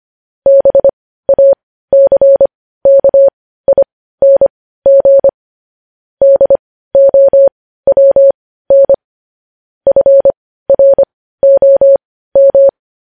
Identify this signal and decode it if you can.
Morse code, heard as a keyed tone, about 25 wpm, spaced as for 15 wpm, 550 Hz. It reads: BACKING DOWN FROM
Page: news-headlines-in-morse-code-at-15-wpm